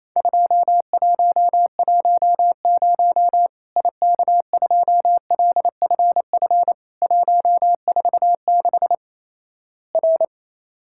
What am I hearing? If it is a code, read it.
Morse code, heard as a keyed tone, about 27 wpm, 700 Hz. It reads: 2110 IK2LFF 146 R
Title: Worked All Europe DX Contest - QTC Exchange - 28wpm